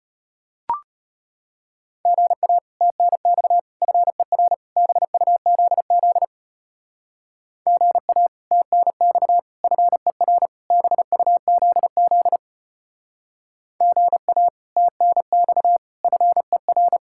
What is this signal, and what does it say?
Morse code, heard as a keyed tone, about 31 wpm, 700 Hz. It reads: GA TNX FER BUZZ GA TNX FER BUZZ GA TNX FER
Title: Speed-Racing Common QSO Phrases - 25wpm